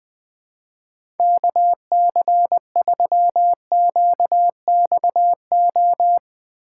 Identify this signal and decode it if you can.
Morse code, heard as a keyed tone, about 20 wpm, 700 Hz. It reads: KC3QXO
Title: 2023 POTA Activator Call Signs - 20wpm